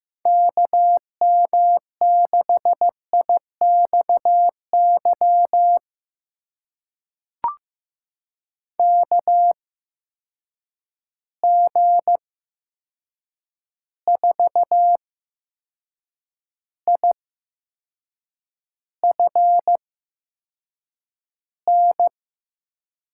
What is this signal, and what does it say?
Morse code, heard as a keyed tone, about 15 wpm, spaced as for 6 wpm, 700 Hz. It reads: KM6IXY KG4IFN
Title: USA Call Signs - 6 Characters - 8x Character Spacing - 15wpm